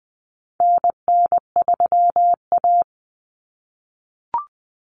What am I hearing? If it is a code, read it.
Morse code, heard as a keyed tone, about 20 wpm, 700 Hz. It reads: NN3A
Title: Common Cut Numbers - 4-Digit 20wpm